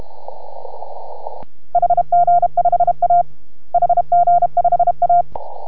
Morse code, heard as a keyed tone, about 33 wpm, 690 Hz. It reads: HG5A HG5A